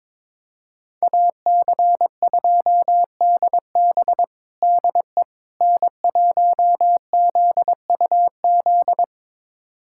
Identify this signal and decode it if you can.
Morse code, heard as a keyed tone, about 21 wpm, 700 Hz. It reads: AC2DB DE N1ZUZ
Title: USA Callsign DE Callsign - 5 Characters - 22wpm